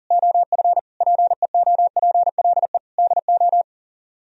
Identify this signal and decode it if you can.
Morse code, heard as a keyed tone, about 52 wpm, 700 Hz. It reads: OF PEOPLE DO